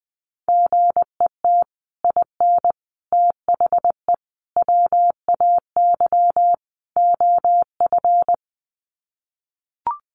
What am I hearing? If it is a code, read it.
Morse code, heard as a keyed tone, about 20 wpm, 700 Hz. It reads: GET IN THE WAY OF
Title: Sets of 5 Words Encore 20wpm